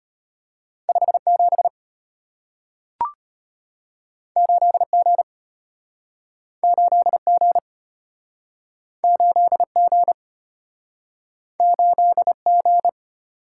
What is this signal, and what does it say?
Morse code, heard as a keyed tone, about 34 wpm, 700 Hz. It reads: 57 8G 8G 8G 8G